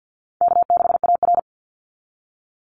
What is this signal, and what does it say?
Morse code, heard as a keyed tone, about 50 wpm, 700 Hz. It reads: K6AR
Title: Call Signs Repeated 2x - 50wpm